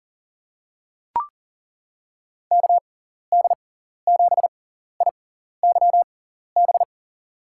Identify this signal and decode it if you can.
Morse code, heard as a keyed tone, about 40 wpm, spaced as for 14 wpm, 700 Hz. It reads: KD7IYB
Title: USA Call Signs - 6 Characters - 6x Character Spacing - 40wpm